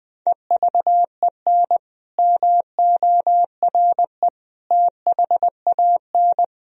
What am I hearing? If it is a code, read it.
Morse code, heard as a keyed tone, about 20 wpm, 700 Hz. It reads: EVEN MORE THAN
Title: Sentences from Top 1000 Words - Review Part 1 - 20wpm